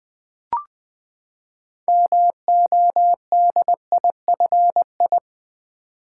Morse code, heard as a keyed tone, about 20 wpm, 700 Hz. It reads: MODIFI
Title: Build A Word - 6 Letters At A Time - 12-Letter Words - 20wpm